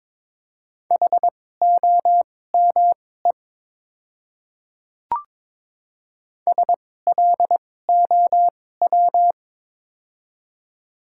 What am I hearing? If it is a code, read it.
Morse code, heard as a keyed tone, about 22 wpm, spaced as for 16 wpm, 700 Hz. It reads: HOME SLOW
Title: 2x Character Spacing - Top 500 Words - 22wpm